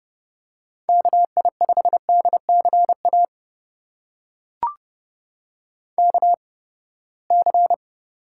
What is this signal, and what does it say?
Morse code, heard as a keyed tone, about 30 wpm, spaced as for 21 wpm, 700 Hz. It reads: KI5DCA KC
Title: USA Call Signs - 8x Character Spacing - 30wpm